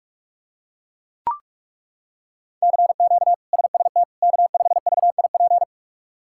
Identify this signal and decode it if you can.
Morse code, heard as a keyed tone, about 41 wpm, 700 Hz. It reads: CQ SST K5VIP
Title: K1USN SST QSO Practice - 45wpm